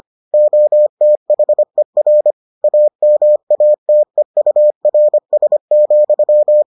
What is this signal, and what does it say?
Morse code, heard as a keyed tone, about 25 wpm, 600 Hz. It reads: OTHER AMATEURS,